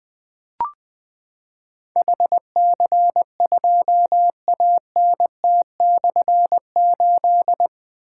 Morse code, heard as a keyed tone, about 20 wpm, 700 Hz. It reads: HC2ANT/8
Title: Difficult Call Signs 20wpm